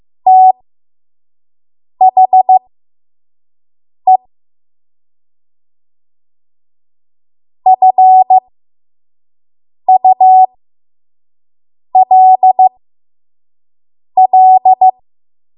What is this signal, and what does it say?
Morse code, heard as a keyed tone, about 15 wpm, spaced as for 5 wpm, 750 Hz. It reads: THE FULL
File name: Morse / 05WPM